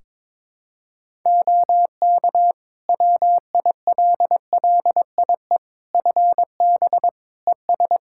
Morse code, heard as a keyed tone, about 22 wpm, 700 Hz. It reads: OK WILLIE FB ES
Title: Practice QSO - Exchange 2 - Info - 22wpm